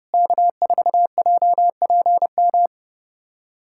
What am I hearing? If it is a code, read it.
Morse code, heard as a keyed tone, about 30 wpm, 700 Hz. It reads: K4JPM
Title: USA Callsign DE Callsign - 5 Characters - 30wpm